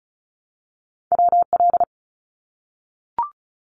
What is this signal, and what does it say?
Morse code, heard as a keyed tone, about 36 wpm, 700 Hz. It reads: WL
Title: Abbreviations 35wpm